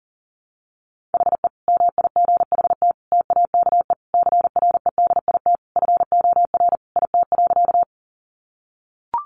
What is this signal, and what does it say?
Morse code, heard as a keyed tone, about 40 wpm, 700 Hz. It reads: HE MIGHT TAKE CREDIT FOR IT.
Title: Sentences from Top 1000 Words 40wpm